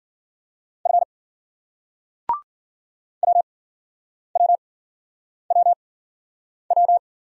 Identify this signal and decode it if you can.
Morse code, heard as a keyed tone, about 50 wpm, spaced as for 9 wpm, 700 Hz. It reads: F WWWW
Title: Speed-Racing Single Letters 40wpm